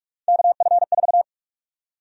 Morse code, heard as a keyed tone, about 46 wpm, 700 Hz. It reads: KF4
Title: USA Call Sign Prefixes - 45wpm